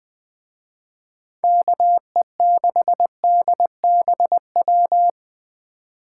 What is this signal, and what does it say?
Morse code, heard as a keyed tone, about 20 wpm, 700 Hz. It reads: KE6DBW